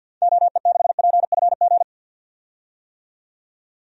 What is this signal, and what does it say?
Morse code, heard as a keyed tone, about 50 wpm, 700 Hz. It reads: OE6PFG
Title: DXCC-100 Call Signs - 6 Characters - 50wpm